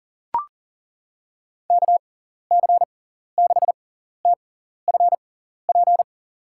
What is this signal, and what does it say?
Morse code, heard as a keyed tone, about 38 wpm, spaced as for 14 wpm, 700 Hz. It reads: KC6TFP
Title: USA Call Signs - 6 Characters - 6x Character Spacing - 40wpm